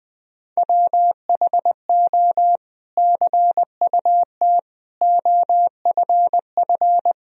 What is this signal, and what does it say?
Morse code, heard as a keyed tone, about 20 wpm, 700 Hz. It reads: WHO CUT OFF
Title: Sentences from Top 1000 Words - Review Part 1 - 20wpm